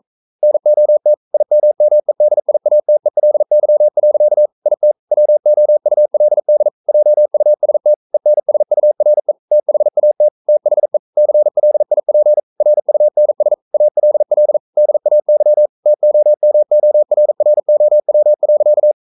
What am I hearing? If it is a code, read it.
Morse code, heard as a keyed tone, about 42 wpm, 600 Hz. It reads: NOT IMMEDIATELY. IT WOULD JUST ENSURE THAT THE CLIP RUNS ALL DAY TOMORROW.
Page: Romanen Never af Ken Follet på engelsk.